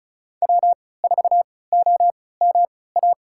Morse code, heard as a keyed tone, about 36 wpm, spaced as for 20 wpm, 690 Hz. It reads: W4OMA
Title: USA Call Signs - 5 Characters - 3x Character Spacing - 35wpm